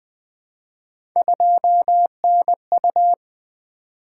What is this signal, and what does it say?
Morse code, heard as a keyed tone, about 20 wpm, 700 Hz. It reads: 2NU
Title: Three Letter-Number Combinations - 20wpm